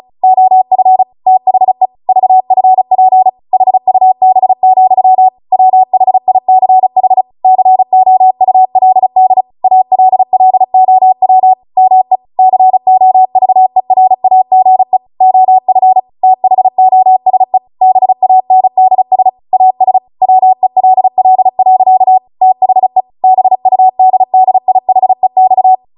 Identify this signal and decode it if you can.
Morse code, heard as a keyed tone, about 36 wpm, 750 Hz. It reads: OF THE VFP HUB, WHICH COULD ALLOW ME COVERAGE OF THOSE BANDS AS WELL. THE BUDDIHEX